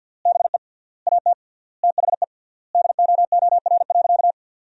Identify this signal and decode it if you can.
Morse code, heard as a keyed tone, about 50 wpm, 700 Hz. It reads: BE AT THE DOOR.
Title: 3x Word-Spacing - Sentences from Top 400 Words - 50wpm